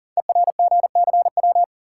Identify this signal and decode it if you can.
Morse code, heard as a keyed tone, about 39 wpm, 700 Hz. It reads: ERGCW
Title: Code Groups - 5 Letter:Numbers - 40wpm